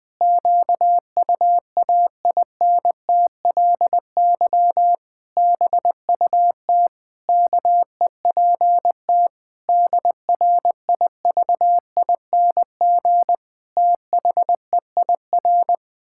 Morse code, heard as a keyed tone, about 21 wpm, 700 Hz. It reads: QUAINTLY BUT KEPT DRIVING THEIR
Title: Pangram Challenge v2 - 20wpm